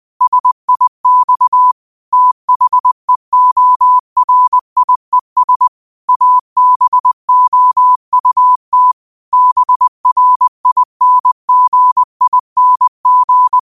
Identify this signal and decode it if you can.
Morse code, heard as a keyed tone, about 20 wpm, 1000 Hz. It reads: SIX THEORIES ABOUT BRINGING